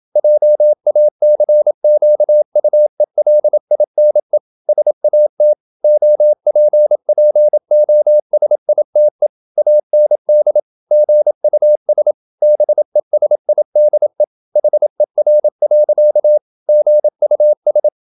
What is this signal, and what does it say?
Morse code, heard as a keyed tone, about 27 wpm, 600 Hz. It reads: JACQUELINE SAT OPPOSITE AND GUS BESIDE HER. GUS
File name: Never 0027wpm